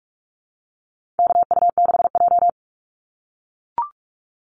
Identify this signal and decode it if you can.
Morse code, heard as a keyed tone, about 44 wpm, 700 Hz. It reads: KU6J
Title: NA SOTA Activator Callsigns 45wpm